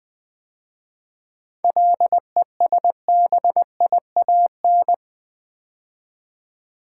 Morse code, heard as a keyed tone, about 20 wpm, 700 Hz. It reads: LESBIAN
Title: 7 Letter Words - First 3 Letters Given - 20wpm